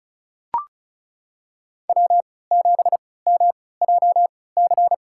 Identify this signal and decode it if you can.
Morse code, heard as a keyed tone, about 34 wpm, spaced as for 20 wpm, 700 Hz. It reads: W7MJC
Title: USA Call Signs - 5 Characters - 3x Character Spacing - 35wpm